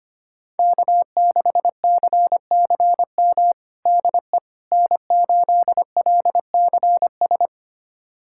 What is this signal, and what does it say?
Morse code, heard as a keyed tone, about 25 wpm, 700 Hz. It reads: K6CCM DE N8LCS
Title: USA Callsign DE Callsign - 5 Characters - 25wpm